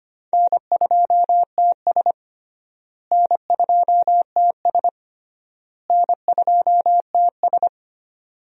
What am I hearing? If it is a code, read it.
Morse code, heard as a keyed tone, about 25 wpm, 700 Hz. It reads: N2TS N2TS N2TS
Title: USA Call Signs - 4 Characters - Repeated 3x - 25wpm